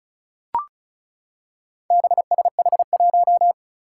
Morse code, heard as a keyed tone, about 36 wpm, 700 Hz. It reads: BSH1
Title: Four Challenging Letter-Number Combinations 35wpm